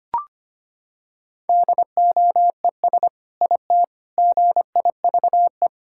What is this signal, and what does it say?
Morse code, heard as a keyed tone, about 25 wpm, 700 Hz. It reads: DOES IT GIVE
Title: Sentences from Top 1600 Words 25wpm